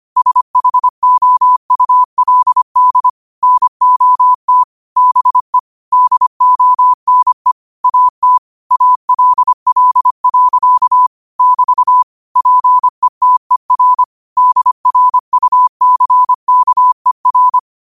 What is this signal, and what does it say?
Morse code, heard as a keyed tone, about 25 wpm, 1000 Hz. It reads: SHOULD NOT BE DONE AT ALL. <BT> PETER DRUCKER
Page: New quotes every day in morse code at 25 Words per minute.